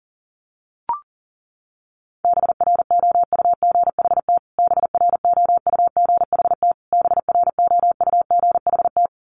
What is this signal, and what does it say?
Morse code, heard as a keyed tone, about 40 wpm, 700 Hz. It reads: BROUGHT BROUGHT BROUGHT
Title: Top 500 Words Repeated 40wpm